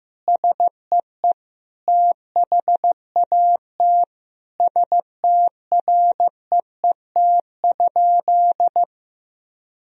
Morse code, heard as a keyed tone, about 16 wpm, 700 Hz. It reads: SEE THAT STREET?
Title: Sentences from Top 1000 Words - Review Part 3 - 15wpm